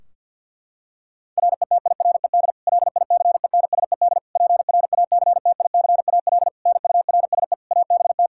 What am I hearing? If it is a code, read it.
Morse code, heard as a keyed tone, about 49 wpm, 700 Hz. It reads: RETIRED LICENSED PRACTICAL NURSE ABT
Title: Practice QSO - Exchange 3 - More Info - 50wpm